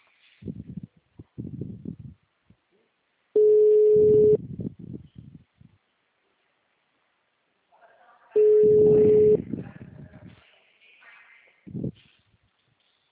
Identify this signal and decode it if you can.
Ringback tone (Europe, 425 Hz).